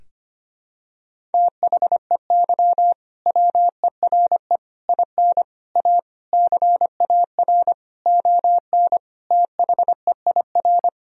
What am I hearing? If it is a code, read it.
Morse code, heard as a keyed tone, about 25 wpm, 700 Hz. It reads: THEY WERE IN A CAR ON THEIR
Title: Sentences from Top 1000 Words - Review Part 1 - 25wpm